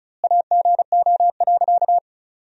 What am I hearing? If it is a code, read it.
Morse code, heard as a keyed tone, about 34 wpm, 700 Hz. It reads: AGO.
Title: 2x Word-Spacing - Sentences from Top 500 Words - 35wpm